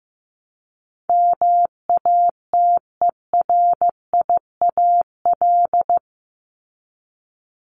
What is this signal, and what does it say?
Morse code, heard as a keyed tone, about 15 wpm, 700 Hz. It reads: MATERIAL
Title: ES:S Suffix - Top 100 Words with Base Words 15wpm